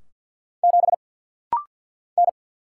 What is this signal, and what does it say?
Morse code, heard as a keyed tone, about 49 wpm, 690 Hz. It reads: 7 N
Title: Single Letter-Number - Rapid-Fire 50wpm